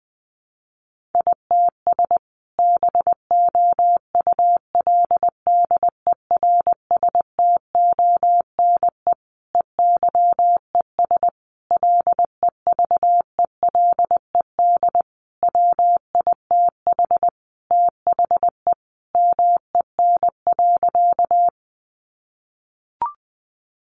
Morse code, heard as a keyed tone, about 20 wpm, 700 Hz. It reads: ITS BOULDERSTONE EYES LEVELED WITH THE MEN.
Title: A Sound Of Thunder 20wpm